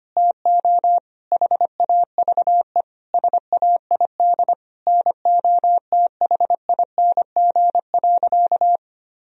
Morse code, heard as a keyed tone, about 25 wpm, 700 Hz. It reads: TO HAVE SAID NOTHING.
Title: Sentences from Top 1000 Words - Review Part 4 - 25wpm